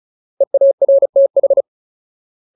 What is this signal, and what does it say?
Morse code, heard as a keyed tone, about 34 wpm, 550 Hz. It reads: EARTH
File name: Ultimate Word List / 35WPM Ultimate